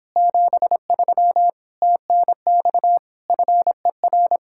Morse code, heard as a keyed tone, about 29 wpm, 700 Hz. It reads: 73 TNX FER